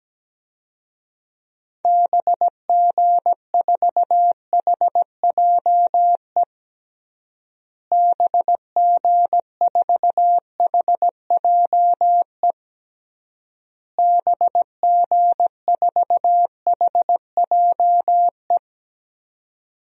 Morse code, heard as a keyed tone, about 17 wpm, 700 Hz. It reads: BG4HJE BG4HJE BG4HJE
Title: DXCC-100 Call Signs - 6 Characters - Repeated 3x - 17wpm